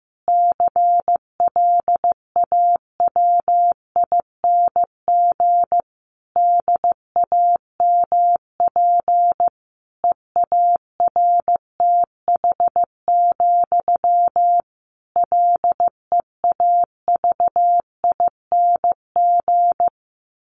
Morse code, heard as a keyed tone, about 15 wpm, 700 Hz. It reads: CLAWING DAMP EARTH, LEAVING
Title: A Sound Of Thunder 15wpm